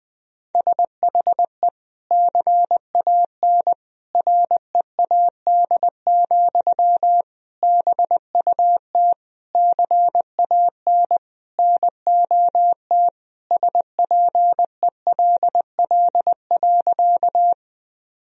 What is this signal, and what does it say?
Morse code, heard as a keyed tone, about 21 wpm, 700 Hz. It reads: SHE CAN READ, BUT CAN NOT SPELL.
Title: Sentences from Top 1000 Words - Review Part 1 - 20wpm